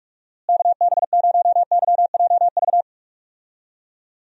Morse code, heard as a keyed tone, about 42 wpm, 700 Hz. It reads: KB0YJV
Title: USA Call Signs - 6 Characters - Repeated 2x - 45wpm